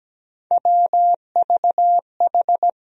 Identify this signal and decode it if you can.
Morse code, heard as a keyed tone, about 17 wpm, 700 Hz. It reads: WVH
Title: USA Call Sign Postfixes - 3 Letters 17wpm